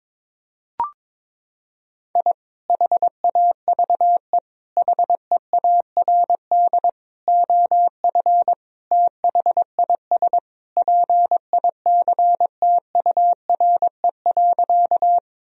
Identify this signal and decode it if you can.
Morse code, heard as a keyed tone, about 22 wpm, 700 Hz. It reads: I HAVE HEARD OF THIS PICTURE.
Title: Sentences from Top 1000 Words - Review Part 1 - 22wpm